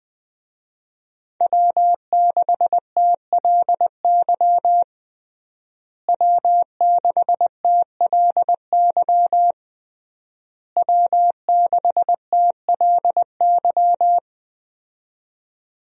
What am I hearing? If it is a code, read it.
Morse code, heard as a keyed tone, about 20 wpm, 700 Hz. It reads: W6TLY W6TLY W6TLY
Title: USA Call Signs - 5 Characters - Repeated 3x - 20wpm